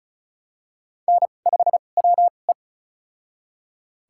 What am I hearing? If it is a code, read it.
Morse code, heard as a keyed tone, about 36 wpm, spaced as for 25 wpm, 700 Hz. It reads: N5WE
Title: USA Call Signs - 2x Character Spacing - 35wpm